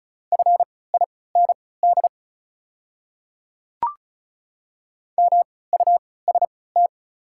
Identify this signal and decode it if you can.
Morse code, heard as a keyed tone, about 34 wpm, spaced as for 20 wpm, 690 Hz. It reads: FIND MUST
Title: 3x Character Spacing - Top 200 Words - 35wpm